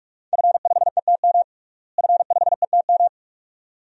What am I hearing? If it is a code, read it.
Morse code, heard as a keyed tone, about 48 wpm, 700 Hz. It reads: F5ETM F5ETM
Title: DXCC-100 Call Signs - 5 Characters - Repeated 2x - 45wpm